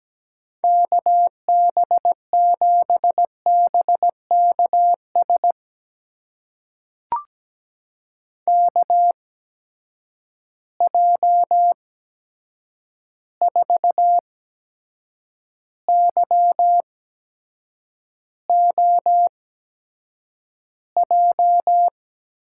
Morse code, heard as a keyed tone, about 17 wpm, spaced as for 7 wpm, 700 Hz. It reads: KB7BKS KJ4YOJ
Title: USA Call Signs - 6 Characters - 8x Character Spacing - 17wpm